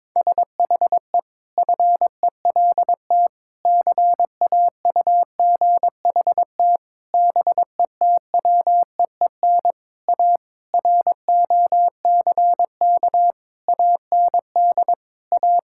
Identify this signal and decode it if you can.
Morse code, heard as a keyed tone, about 22 wpm, 700 Hz. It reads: SHE FELT CAUGHT BETWEEN A ROCK AND A
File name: practice / 22wpm / Sentences with 8-Word Phrases - 22wpm